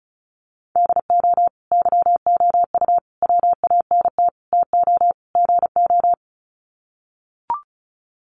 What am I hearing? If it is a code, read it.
Morse code, heard as a keyed tone, about 34 wpm, 700 Hz. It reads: DO YOU WANT TO GO
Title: Sets of 5 Words Encore2 35wpm